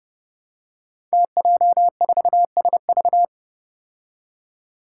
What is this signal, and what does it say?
Morse code, heard as a keyed tone, about 30 wpm, 700 Hz. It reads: TJ4SV
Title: Five Challenging Letter-Number Combinations 30wpm